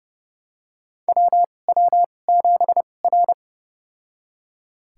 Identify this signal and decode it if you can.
Morse code, heard as a keyed tone, about 29 wpm, spaced as for 21 wpm, 700 Hz. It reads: WW7R
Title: USA Call Signs - 4 Characters - 2x Character Spacing - 30wpm